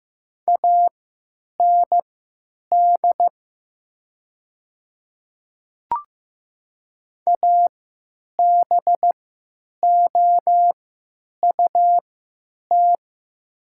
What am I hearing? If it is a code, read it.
Morse code, heard as a keyed tone, about 15 wpm, spaced as for 9 wpm, 700 Hz. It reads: AND ABOUT
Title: 3x Character Spacing - Top 500 Words - 15wpm